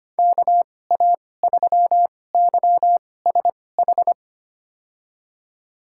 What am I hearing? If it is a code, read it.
Morse code, heard as a keyed tone, about 25 wpm, spaced as for 18 wpm, 700 Hz. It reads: KA3YSH
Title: USA Call Signs - 2x Character Spacing - 25wpm